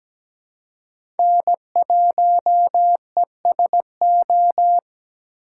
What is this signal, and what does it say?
Morse code, heard as a keyed tone, about 17 wpm, 700 Hz. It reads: N1ESO
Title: USA Call Signs - 5 Characters - 3x Character Spacing - 17wpm